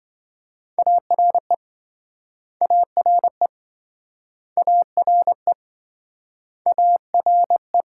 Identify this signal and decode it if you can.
Morse code, heard as a keyed tone, about 27 wpm, 700 Hz. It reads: ARE ARE ARE ARE